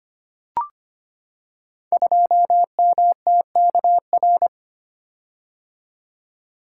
Morse code, heard as a keyed tone, about 25 wpm, 700 Hz. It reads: 2MTKR
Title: Code Groups - 5 Letter:Numbers - 25wpm